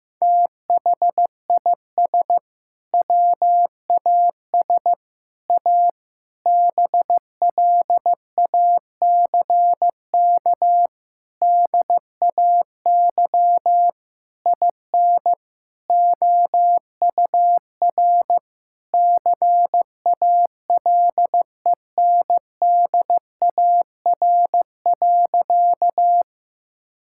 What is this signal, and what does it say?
Morse code, heard as a keyed tone, about 15 wpm, 700 Hz. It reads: THIS WAS A BLACK DAY IN OUR CALENDAR.
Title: Sentences from Top 1600 Words 15wpm